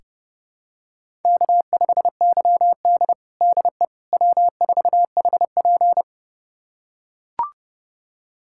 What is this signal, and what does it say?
Morse code, heard as a keyed tone, about 30 wpm, 700 Hz. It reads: K5YD DE W4HP
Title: USA Callsign DE Callsign - 4 Characters - 30wpm